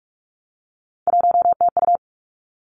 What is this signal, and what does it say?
Morse code, heard as a keyed tone, about 46 wpm, 700 Hz. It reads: 1TU
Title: Three Challenging Letter-Number Combinations 45wpm